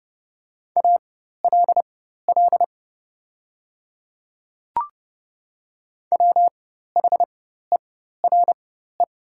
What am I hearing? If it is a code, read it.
Morse code, heard as a keyed tone, about 30 wpm, spaced as for 14 wpm, 700 Hz. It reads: ALL WHERE
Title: 4x Character Spacing - Top 200 Words - 30wpm